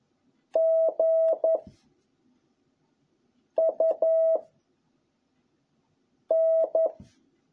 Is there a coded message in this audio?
Yes: Morse code from 0.6 to 6.9 s, heard as a keyed tone, about 11 wpm, 660 Hz. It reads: G U N